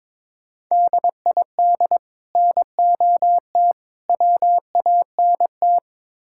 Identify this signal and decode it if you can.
Morse code, heard as a keyed tone, about 22 wpm, 700 Hz. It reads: DID NOT WANT
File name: practice / 22wpm / Build A Phrase - 5-Word Phrases - 22wpm